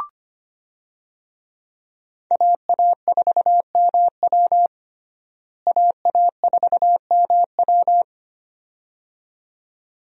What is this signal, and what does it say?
Morse code, heard as a keyed tone, about 25 wpm, 700 Hz. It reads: AA4MW AA4MW
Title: USA Call Signs - 5 Characters - Repeated 2x - 25wpm